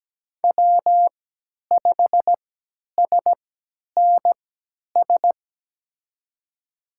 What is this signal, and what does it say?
Morse code, heard as a keyed tone, about 17 wpm, spaced as for 10 wpm, 700 Hz. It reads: W5SNS
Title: USA Call Signs - 5 Characters - 3x Character Spacing - 17wpm